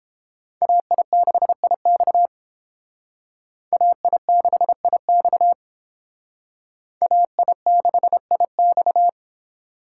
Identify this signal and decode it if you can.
Morse code, heard as a keyed tone, about 29 wpm, 700 Hz. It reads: AI6IX AI6IX AI6IX